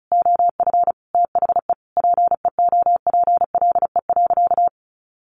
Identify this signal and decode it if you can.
Morse code, heard as a keyed tone, about 34 wpm, 700 Hz. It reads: OF THE PEOPLE.
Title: Sentences from Top 1500 Words 35wpm